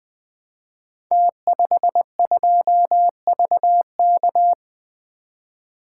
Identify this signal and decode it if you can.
Morse code, heard as a keyed tone, about 20 wpm, 700 Hz. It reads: T52VK
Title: Code Groups - 5 Letter:Numbers - 20wpm